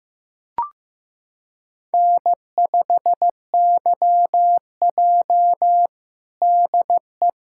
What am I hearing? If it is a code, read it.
Morse code, heard as a keyed tone, about 15 wpm, 700 Hz. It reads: N5YJ DE
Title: USA Callsign DE Callsign - 4 Characters - 15wpm